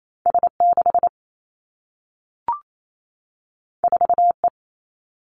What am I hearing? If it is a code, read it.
Morse code, heard as a keyed tone, about 28 wpm, 700 Hz. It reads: S 6 4 E
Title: Two Challenging Letter-Number Combinations 28wpm